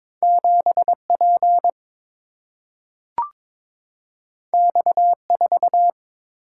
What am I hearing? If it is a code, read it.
Morse code, heard as a keyed tone, about 22 wpm, 700 Hz. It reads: 7P X4
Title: Single-Letter Single-Number 22wpm